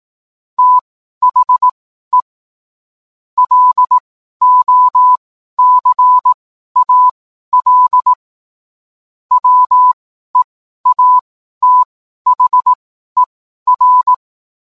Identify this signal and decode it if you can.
Morse code, heard as a keyed tone, about 18 wpm, spaced as for 12 wpm, 1000 Hz. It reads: THE LOCAL WEATHER
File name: cwblog / cw mp3 directory / mem morse 12